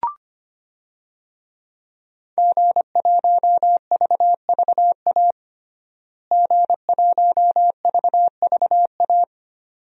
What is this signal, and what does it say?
Morse code, heard as a keyed tone, about 25 wpm, 700 Hz. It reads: G1VVA G1VVA
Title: DXCC-100 Call Signs - 5 Characters - Repeated 2x - 25wpm